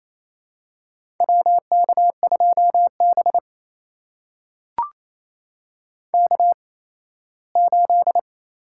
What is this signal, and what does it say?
Morse code, heard as a keyed tone, about 27 wpm, spaced as for 17 wpm, 700 Hz. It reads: WK2B K8
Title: USA Call Signs - 4 Characters - 8x Character Spacing - 28wpm